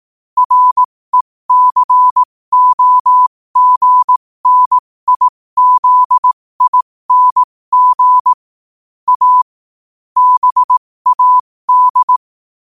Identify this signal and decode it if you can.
Morse code, heard as a keyed tone, about 18 wpm, spaced as for 15 wpm, 1000 Hz. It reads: RECOGNIZING A BAD
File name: qotdfiles / 15wpm